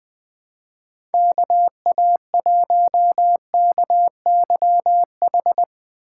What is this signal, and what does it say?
Morse code, heard as a keyed tone, about 20 wpm, 700 Hz. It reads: KA1KYH
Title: USA Call Signs - 3x Character Spacing - 20wpm